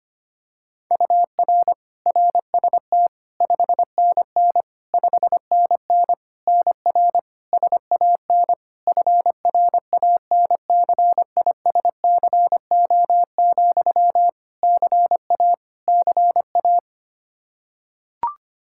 Morse code, heard as a keyed tone, about 25 wpm, 700 Hz. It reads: UR RST 5NN 5NN NR SAN FRANCISCO, CA CA
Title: Practice QSO - Exchange 1 - Beginning - 25wpm